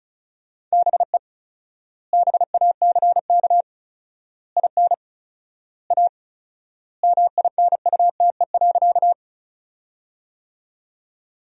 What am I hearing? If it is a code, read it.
Morse code, heard as a keyed tone, about 36 wpm, 700 Hz. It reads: BE BACK IN A MINUTE.
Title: 4x Word-Spacing - Sentences from Top 500 Words - 35wpm